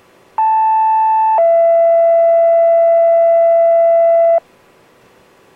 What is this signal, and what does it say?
Two-tone page, heard as a one-second tone, then a three-second tone, about 880 then 670 Hz. It is heard from 0.4 to 4.4 s.